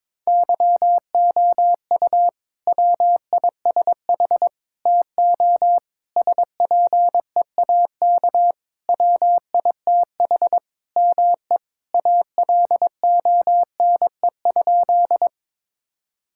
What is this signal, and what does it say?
Morse code, heard as a keyed tone, about 22 wpm, 700 Hz. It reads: YOU WISH TO SPEAK WITH ME ALONE?
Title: Sentences from Top 1600 Words 22wpm